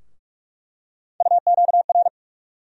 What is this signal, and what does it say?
Morse code, heard as a keyed tone, about 46 wpm, 700 Hz. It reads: UQR
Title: USA Call Sign Postfixes - 3 Letters 45wpm